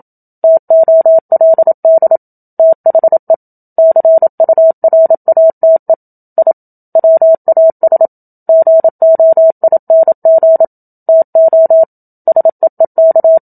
Morse code, heard as a keyed tone, about 27 wpm, 650 Hz. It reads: TOLD THE CURATE I WAS GOING TO SEEK
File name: Book Two / 01-Chapter 1 - 27 WPM